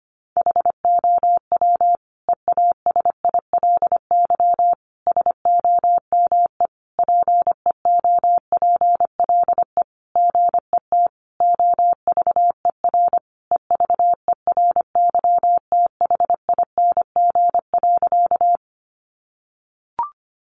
Morse code, heard as a keyed tone, about 25 wpm, 700 Hz. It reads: HOW EASILY SOME PEOPLE GET OVER EVERYTHING.
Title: Sentences from Top 900 Words 25wpm